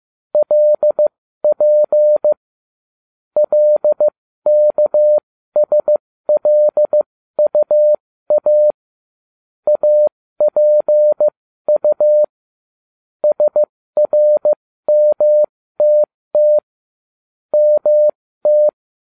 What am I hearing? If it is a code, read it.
Morse code, heard as a keyed tone, about 15 wpm, spaced as for 13 wpm, 600 Hz. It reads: LP LKSLUA APU SRMTT MT
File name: cw / ubacw08 2